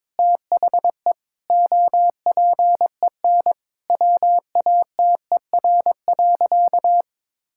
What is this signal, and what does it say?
Morse code, heard as a keyed tone, about 22 wpm, 700 Hz. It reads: THE OPEN WATER.
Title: Sentences from Top 1000 Words - Review Part 6 - 22wpm